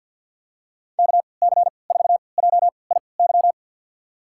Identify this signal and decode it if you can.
Morse code, heard as a keyed tone, about 49 wpm, spaced as for 28 wpm, 690 Hz. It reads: KC4JIY
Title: USA Call Signs - 6 Characters - 3x Character Spacing - 50wpm